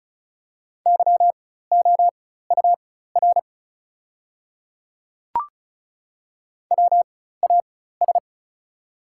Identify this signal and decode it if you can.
Morse code, heard as a keyed tone, about 36 wpm, spaced as for 16 wpm, 700 Hz. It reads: YOUR WAS
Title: 4x Character Spacing - Top 100 Words - 35wpm